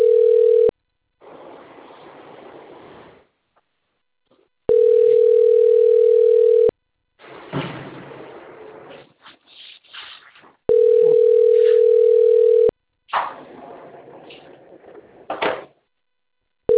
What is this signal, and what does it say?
Ringback tone (North America), 440 + 480 Hz.